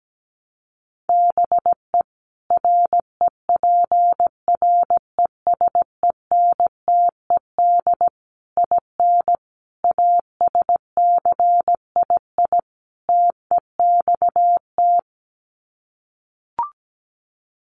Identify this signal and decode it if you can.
Morse code, heard as a keyed tone, about 17 wpm, 700 Hz. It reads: BE REPRESENTED IN ASCII TEXT
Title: Sets of 5 Words 17wpm